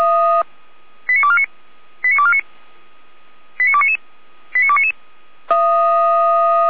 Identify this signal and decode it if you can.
Five-tone selective call (ZVEI), heard as quick runs of five tones, tones of about 80 ms. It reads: 89249 89249 89299 89299
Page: SDR/ZVEI